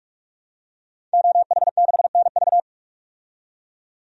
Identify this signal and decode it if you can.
Morse code, heard as a keyed tone, about 44 wpm, 700 Hz. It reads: OH6NV
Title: DXCC-100 Call Signs - 5 Characters - 45wpm